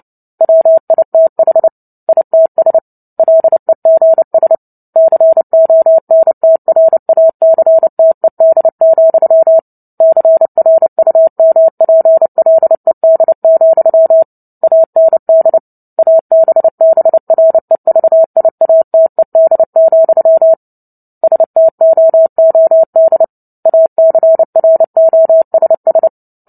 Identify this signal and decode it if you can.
Morse code, heard as a keyed tone, about 30 wpm, 650 Hz. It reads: WITH ITS LEGS CONTRACTED, CRUMPLED, AND ABBREVIATED, STOOD ACROSS
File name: Book Two / 03-Chapter 3 - 29 WPM